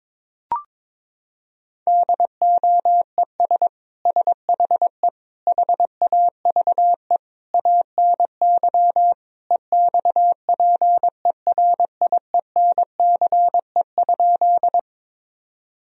Morse code, heard as a keyed tone, about 22 wpm, 700 Hz. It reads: DOES SHE HAVE ANY EXPERIENCE?
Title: Sentences from Top 1000 Words - Review Part 3 - 22wpm